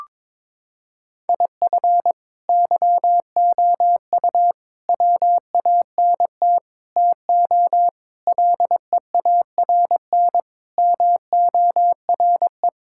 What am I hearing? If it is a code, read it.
Morse code, heard as a keyed tone, about 22 wpm, 700 Hz. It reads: IF YOU WANT TO LEARN MORE
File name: Sentences with 7-Word Phrases - 22wpm